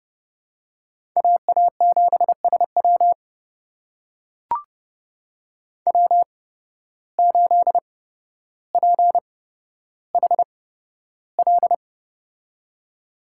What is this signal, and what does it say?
Morse code, heard as a keyed tone, about 31 wpm, spaced as for 13 wpm, 700 Hz. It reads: AA7SW W8PHL
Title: USA Call Signs - 5 Characters - 8x Character Spacing - 30wpm